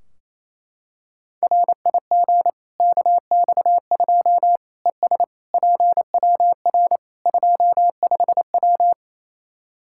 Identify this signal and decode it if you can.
Morse code, heard as a keyed tone, about 27 wpm, 700 Hz. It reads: RIG KX2 ES PWR 25W
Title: Practice QSO - Exchange 2 - Info - 28wpm